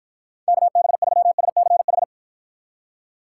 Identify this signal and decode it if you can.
Morse code, heard as a keyed tone, about 52 wpm, 700 Hz. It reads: KB3SCH